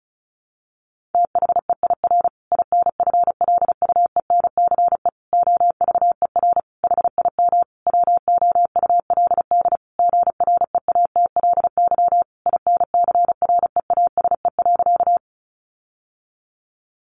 Morse code, heard as a keyed tone, about 34 wpm, 700 Hz. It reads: THEIR INFLUENCE OVER HIM WOULD GREATLY INCREASE.
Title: Sentences from Top 1400 Words 35wpm